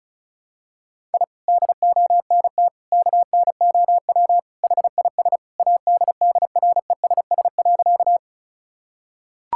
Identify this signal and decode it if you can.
Morse code, heard as a keyed tone, about 36 wpm, 700 Hz. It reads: I DONT KNOW HIS ADDRESS.
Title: Sentences from Top 1000 Words - Review Part 6 - 35wpm